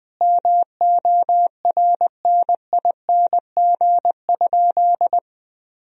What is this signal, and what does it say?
Morse code, heard as a keyed tone, about 20 wpm, 700 Hz. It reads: MORNING?
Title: Sentences from Top 1000 Words - Review Part 2 - 20wpm